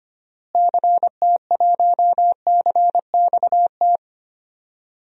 Morse code, heard as a keyed tone, about 25 wpm, 700 Hz. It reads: CT1CXT
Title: DXCC-100 Call Signs - 6 Characters - Repeated 3x - 25wpm